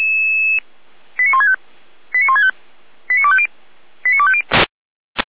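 Five-tone selective call (ZVEI), heard as quick runs of five tones, tones of about 70 ms. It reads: 89165 89165 89250 89250
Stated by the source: ZVEI five-tone call received by SDR.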